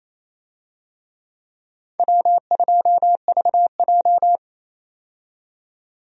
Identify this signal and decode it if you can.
Morse code, heard as a keyed tone, about 28 wpm, 700 Hz. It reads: W2VJ
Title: USA Call Signs - 4 Characters - 28wpm